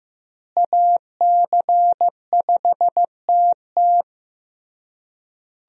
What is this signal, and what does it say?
Morse code, heard as a keyed tone, about 15 wpm, 700 Hz. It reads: AC5TT
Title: USA Call Signs - 5 Characters - Repeated 3x - 15wpm